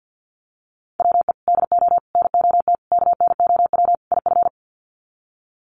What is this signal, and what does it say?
Morse code, heard as a keyed tone, about 50 wpm, 700 Hz. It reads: WE DO NOT KNOW IF
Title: Sets of 5 Words Encore2 50wpm